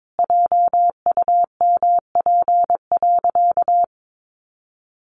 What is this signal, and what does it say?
Morse code, heard as a keyed tone, about 22 wpm, 700 Hz. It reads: JUMP.
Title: Sentences from Top 1300 Words 22wpm